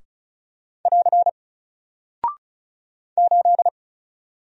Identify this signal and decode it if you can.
Morse code, heard as a keyed tone, about 36 wpm, 700 Hz. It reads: <AR> 8
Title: Single Character (Letter:Number:Punctuation:Prosign) Alt 35wpm